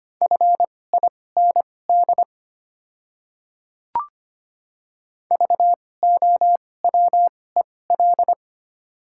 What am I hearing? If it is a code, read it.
Morse code, heard as a keyed tone, about 25 wpm, spaced as for 18 wpm, 700 Hz. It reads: FIND VOWEL
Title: 2x Character Spacing - Top 500 Words - 25wpm